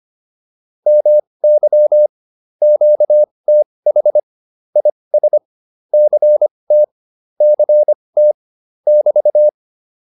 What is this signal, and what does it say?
Morse code, heard as a keyed tone, about 25 wpm, spaced as for 20 wpm, 600 Hz. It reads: MY QTH IS CT CT <BT>
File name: QSO206a 20